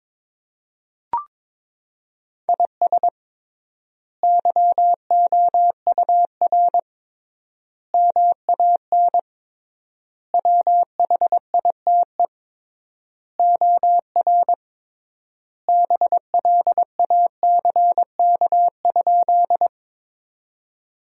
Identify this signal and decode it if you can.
Morse code, heard as a keyed tone, about 22 wpm, 700 Hz. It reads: IS YOUR MAN WHITE OR BLACK?
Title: 3x Word-Spacing - Sentences from Top 400 Words - 22wpm